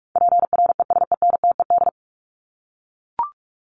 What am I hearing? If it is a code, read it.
Morse code, heard as a keyed tone, about 44 wpm, 700 Hz. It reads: PRESENTED
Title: ED Suffix - Top 100 Words with Base Words 45wpm